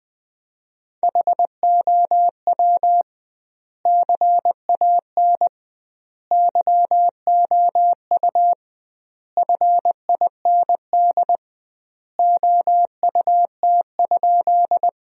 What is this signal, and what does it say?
Morse code, heard as a keyed tone, about 20 wpm, 700 Hz. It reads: HOW CAN YOU FIND OUT?
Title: 2x Word-Spacing - Sentences from Top 200 Words - 20wpm